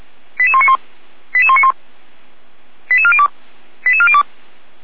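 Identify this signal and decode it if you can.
Five-tone selective call (ZVEI), heard as quick runs of five tones, tones of about 70 ms. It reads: 88191 88191 88402 88402
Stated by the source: ZVEI five-tone call 88235 received by SDR, Feuerwehr Orlamünde.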